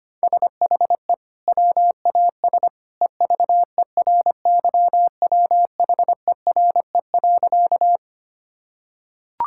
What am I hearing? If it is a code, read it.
Morse code, heard as a keyed tone, about 25 wpm, 700 Hz. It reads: SHE WAS EVERYWHERE.
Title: Sentences from Top 1600 Words 25wpm